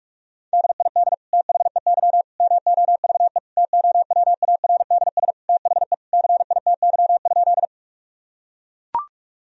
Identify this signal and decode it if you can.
Morse code, heard as a keyed tone, about 41 wpm, 700 Hz. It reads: DID THEY MOVE TOWARDS THE CITY?
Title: Sentences from Top 1000 Words - Review Part 2 - 45wpm